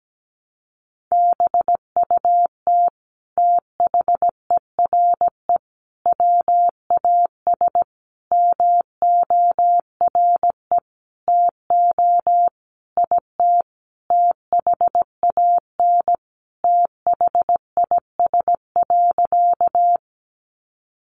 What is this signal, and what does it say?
Morse code, heard as a keyed tone, about 17 wpm, 700 Hz. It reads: BUT THERE WAS MORE TO IT THAN THIS.
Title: Sentences from Top 100 Words 17wpm